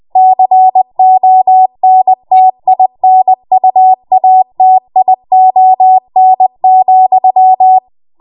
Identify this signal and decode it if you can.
Morse code, heard as a keyed tone, about 20 wpm, 750 Hz. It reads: CONTINUATION,
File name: Morse / 20WPM